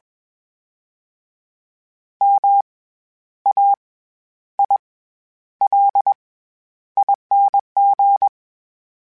Morse code, heard as a keyed tone, about 21 wpm, 800 Hz. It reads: M A I L ING
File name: Special INGs-mp3s / Special INGs-mp3s / SlowToFaster ing @20